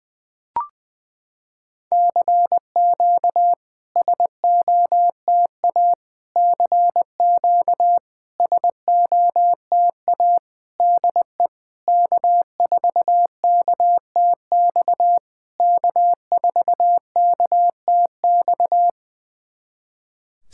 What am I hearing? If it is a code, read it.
Morse code, heard as a keyed tone, about 20 wpm, 700 Hz. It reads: CQ SOTA CQ SOTA DE K4KTX K4KTX
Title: SOTA QSO Practice 20wpm